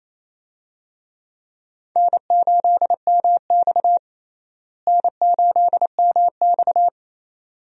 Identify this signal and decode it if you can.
Morse code, heard as a keyed tone, about 28 wpm, 700 Hz. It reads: N8MX N8MX
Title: USA Call Signs - 4 Characters - Repeated 2x - 28wpm